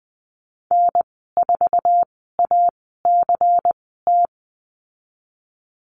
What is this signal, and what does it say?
Morse code, heard as a keyed tone, about 20 wpm, spaced as for 15 wpm, 700 Hz. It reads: N4ACT
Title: 167 Lesson 27g - Number 7 - Call Signs 20wpm